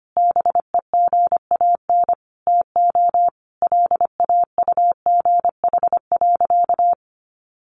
Morse code, heard as a keyed tone, about 25 wpm, 700 Hz. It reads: BEGAN TO LAUGH.
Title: Sentences from Top 600 Words 25wpm